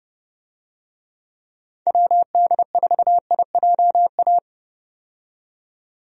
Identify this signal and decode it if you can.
Morse code, heard as a keyed tone, about 30 wpm, 700 Hz. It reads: WD4IJA
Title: USA Call Signs - 6 Characters - 30wpm